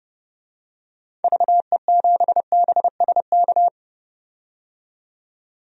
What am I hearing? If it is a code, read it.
Morse code, heard as a keyed tone, about 30 wpm, 700 Hz. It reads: VE7BSK
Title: DXCC-100 Call Signs - 6 Characters - 30wpm